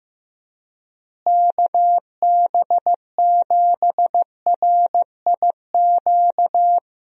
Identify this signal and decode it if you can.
Morse code, heard as a keyed tone, about 15 wpm, 700 Hz. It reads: KB7RIQ
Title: USA Call Signs - 6 Characters - Repeated 3x - 15wpm